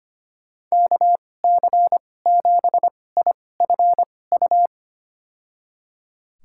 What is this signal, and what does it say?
Morse code, heard as a keyed tone, about 25 wpm, spaced as for 18 wpm, 700 Hz. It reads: KC7IFU
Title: USA Call Signs - 6 Characters - 2x Character Spacing - 25wpm